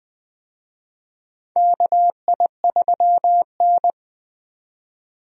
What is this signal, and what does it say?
Morse code, heard as a keyed tone, about 20 wpm, 700 Hz. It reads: KI3N
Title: USA Call Signs - 4 Characters - 20wpm